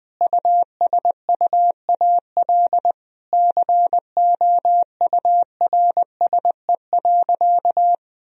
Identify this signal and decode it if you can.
Morse code, heard as a keyed tone, about 20 wpm, 700 Hz. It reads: USUAL COURSE.
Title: Sentences from Top 1000 Words - Review Part 2 - 20wpm